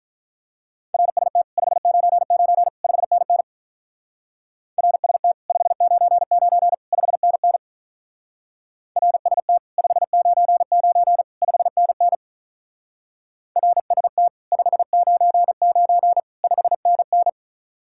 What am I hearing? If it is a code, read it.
Morse code, heard as a keyed tone, about 43 wpm, 700 Hz. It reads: RST 599 5NN RST 599 5NN RST 599 5NN RST 599 5NN
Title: Speed-Racing Common QSO Phrases - 35wpm